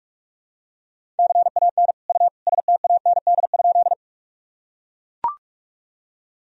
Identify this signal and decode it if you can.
Morse code, heard as a keyed tone, about 41 wpm, 700 Hz. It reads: CAN U STAND?
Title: Sentences from Top 300 Words and Abbreviations 45wpm